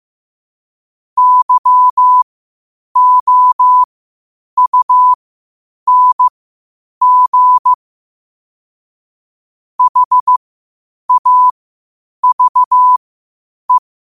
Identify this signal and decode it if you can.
Morse code, heard as a keyed tone, about 15 wpm, spaced as for 8 wpm, 1000 Hz. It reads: YOUNG HAVE